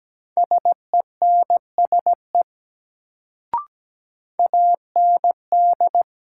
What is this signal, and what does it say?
Morse code, heard as a keyed tone, about 17 wpm, 700 Hz. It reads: SENSE AND
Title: Sets of 2 Words - New Word Drill - 17wpm